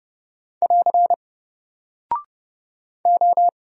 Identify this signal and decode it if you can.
Morse code, heard as a keyed tone, about 30 wpm, 700 Hz. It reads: <AR> O
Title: Single Character (Letter:Number:Punctuation:Prosign) 30wpm